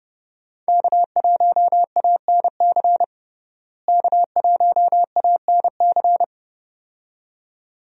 Morse code, heard as a keyed tone, about 30 wpm, 700 Hz. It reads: K1ANC K1ANC
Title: USA Call Signs - 5 Characters - Repeated 2x - 30wpm